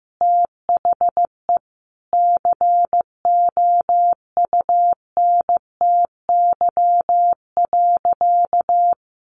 Morse code, heard as a keyed tone, about 15 wpm, 700 Hz. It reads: THE COUNTY.
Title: Sentences from Top 1500 Words 15wpm